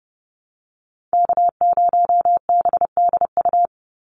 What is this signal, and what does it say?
Morse code, heard as a keyed tone, about 30 wpm, 700 Hz. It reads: K0BDU
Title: USA Call Signs 30wpm